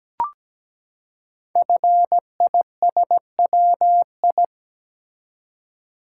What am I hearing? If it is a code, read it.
Morse code, heard as a keyed tone, about 17 wpm, 700 Hz. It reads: FISWI
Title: Code Groups - 5 Letters - 17wpm